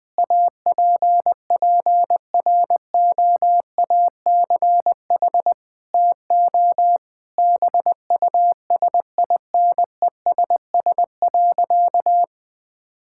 Morse code, heard as a keyed tone, about 21 wpm, 700 Hz. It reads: APPROACH TO BUSINESS.
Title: Sentences from Top 1000 Words - Review Part 6 - 20wpm